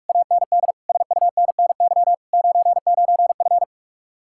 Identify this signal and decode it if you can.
Morse code, heard as a keyed tone, about 62 wpm, spaced as for 54 wpm, 690 Hz. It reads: AND SUNNY 99F
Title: Speed-Racing Common QSO Phrases - 45wpm